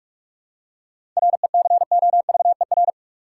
Morse code, heard as a keyed tone, about 44 wpm, 700 Hz. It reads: RECOVER
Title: Seven Letter Words - 45wpm